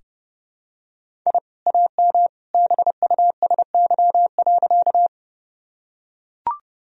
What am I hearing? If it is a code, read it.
Morse code, heard as a keyed tone, about 32 wpm, 700 Hz. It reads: I AM BUSY.
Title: Sentences from Top 1000 Words - Review Part 3 - 30wpm